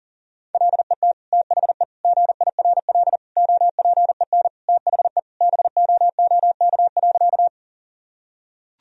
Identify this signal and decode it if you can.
Morse code, heard as a keyed tone, about 40 wpm, 700 Hz. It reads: LET THE GIRL OPEN THE BOOK.
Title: Sentences from Top 1000 Words - Review Part 1 - 40wpm